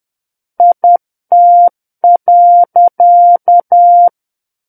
Morse code, heard as a keyed tone, about 10 wpm, 700 Hz. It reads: IT.
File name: Book One / 01-Chapter 1 - 10 WPM